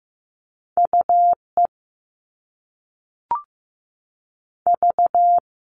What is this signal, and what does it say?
Morse code, heard as a keyed tone, about 15 wpm, 700 Hz. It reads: U E V
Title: All Cut Numbers - 2-Digit 15wpm